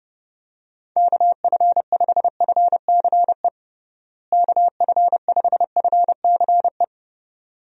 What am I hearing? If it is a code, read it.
Morse code, heard as a keyed tone, about 32 wpm, 700 Hz. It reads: KF5FCE KF5FCE
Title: USA Call Signs - 6 Characters - Repeated 2x - 30wpm